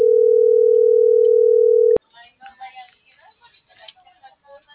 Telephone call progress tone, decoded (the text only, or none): ringback tone